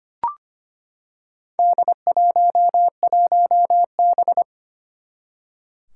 Morse code, heard as a keyed tone, about 25 wpm, 700 Hz. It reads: D11B
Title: Four Challenging Letter-Number Combinations 25wpm